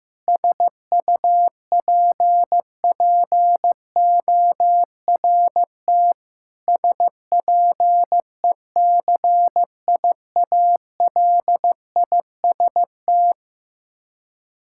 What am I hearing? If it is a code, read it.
Morse code, heard as a keyed tone, about 15 wpm, 700 Hz. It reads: SUPPORT SPECIALIST
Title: Practice QSO - Exchange 3 - More Info - 15wpm